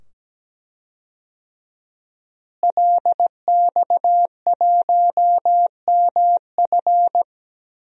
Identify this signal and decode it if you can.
Morse code, heard as a keyed tone, about 17 wpm, 700 Hz. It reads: LX1MF
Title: DXCC-100 Call Signs - 5 Characters - Repeated 2x - 17wpm